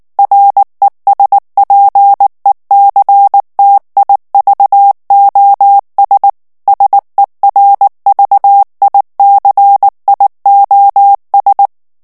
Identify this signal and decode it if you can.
Morse code, heard as a keyed tone, about 19 wpm, 800 Hz. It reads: RESPECTIVOS SERVICIOS
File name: morse mp3 / ESPAGNOL TEXTE EN CLAIR 2 VITESSE 18 MOTS MINUTES